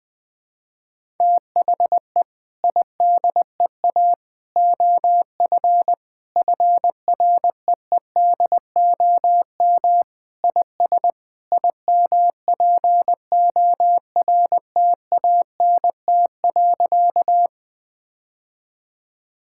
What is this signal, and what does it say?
Morse code, heard as a keyed tone, about 21 wpm, 700 Hz. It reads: THE IDEA OF FREEDOM IS IMPORTANT.
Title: Sentences with 3-Word Phrases - 20wpm